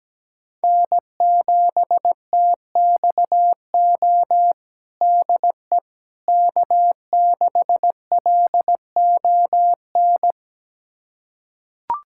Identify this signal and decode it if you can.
Morse code, heard as a keyed tone, about 17 wpm, 700 Hz. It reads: N7TXO DE K6LON
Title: USA Callsign DE Callsign - 5 Characters - 17wpm